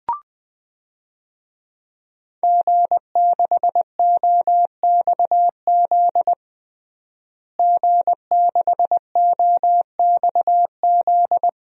Morse code, heard as a keyed tone, about 20 wpm, 700 Hz. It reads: G6OXZ G6OXZ
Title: DXCC-100 Call Signs - 5 Characters - Repeated 3x - 20wpm